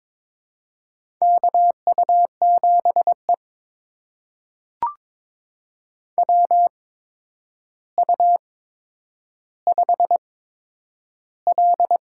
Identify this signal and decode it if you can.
Morse code, heard as a keyed tone, about 21 wpm, spaced as for 9 wpm, 700 Hz. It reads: KU7E WU5L
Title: USA Call Signs - 8x Character Spacing - 22wpm